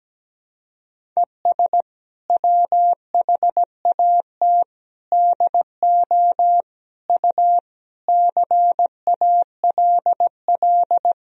Morse code, heard as a keyed tone, about 17 wpm, 700 Hz. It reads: ES WHAT DO U CALL
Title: Sentences from Top 100 Words and Abbreviations 17wpm